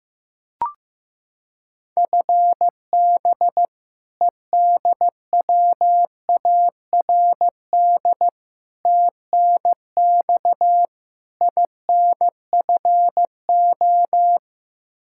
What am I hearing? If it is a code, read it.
Morse code, heard as a keyed tone, about 15 wpm, 700 Hz. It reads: FB EDWARD TNX INFO
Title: Practice QSO - Exchange 3 - More Info - 15wpm